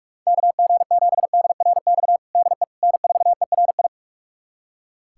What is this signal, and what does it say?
Morse code, heard as a keyed tone, about 43 wpm, 700 Hz. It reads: KG7DRX DE N4ERI
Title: USA Callsign DE Callsign 45wpm